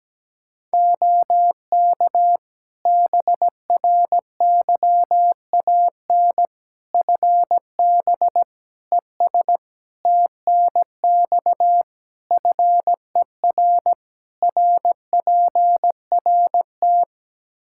Morse code, heard as a keyed tone, about 17 wpm, 700 Hz. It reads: OK BRYAN FB ES TNX FER RPRT
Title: Practice QSO - Exchange 2 - Info - 17wpm